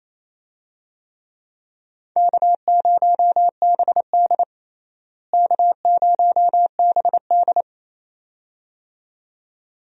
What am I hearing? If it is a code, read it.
Morse code, heard as a keyed tone, about 28 wpm, 700 Hz. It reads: K0BD K0BD
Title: USA Call Signs - 4 Characters - Repeated 2x - 28wpm